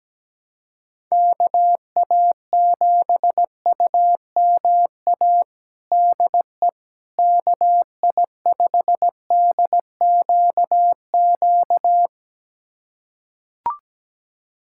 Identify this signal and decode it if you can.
Morse code, heard as a keyed tone, about 17 wpm, 700 Hz. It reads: KA7UMA DE KI5DQQ
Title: USA Callsign DE Callsign - 6 Characters - 17wpm